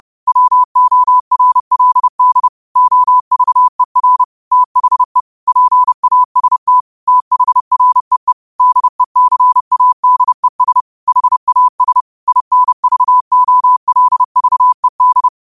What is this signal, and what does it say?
Morse code, heard as a keyed tone, about 30 wpm, 1000 Hz. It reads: WORLD OVER THE PAST THREE DECADES HAS INVOLVED